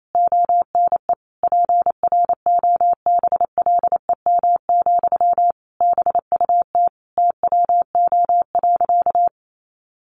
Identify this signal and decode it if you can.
Morse code, heard as a keyed tone, about 28 wpm, 700 Hz. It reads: ONE PROBLEM, BUT TWO.
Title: Sentences from Top 1400 Words 28wpm